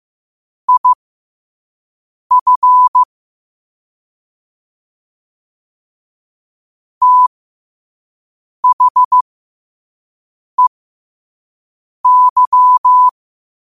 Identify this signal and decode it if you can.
Morse code, heard as a keyed tone, about 15 wpm, spaced as for 5 wpm, 1000 Hz. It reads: IF THEY
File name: qotdfiles / 5wpm